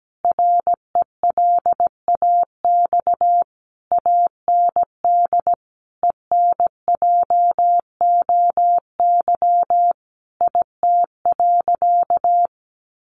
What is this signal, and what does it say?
Morse code, heard as a keyed tone, about 17 wpm, 700 Hz. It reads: RELAX AND ENJOY IT.
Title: Sentences from Top 1400 Words 17wpm